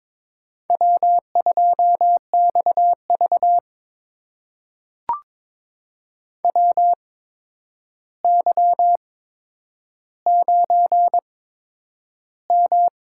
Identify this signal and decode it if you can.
Morse code, heard as a keyed tone, about 22 wpm, spaced as for 9 wpm, 700 Hz. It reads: W2XV WY9M
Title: USA Call Signs - 4 Characters - 8x Character Spacing - 22wpm